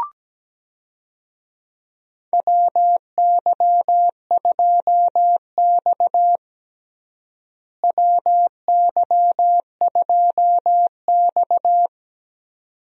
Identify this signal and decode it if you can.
Morse code, heard as a keyed tone, about 17 wpm, 700 Hz. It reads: WY2X WY2X
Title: USA Call Signs - 4 Characters - Repeated 2x - 17wpm